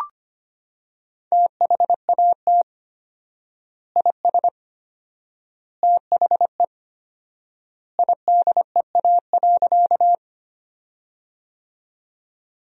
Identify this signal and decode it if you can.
Morse code, heard as a keyed tone, about 25 wpm, 700 Hz. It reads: THAT IS THE IDEA.
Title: 4x Word-Spacing - Sentences from Top 300 Words - 25wpm